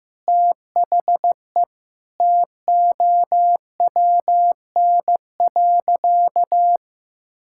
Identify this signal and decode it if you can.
Morse code, heard as a keyed tone, about 15 wpm, 700 Hz. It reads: THE TOWN.
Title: Sentences from Top 1000 Words - Review Part 5 - 15wpm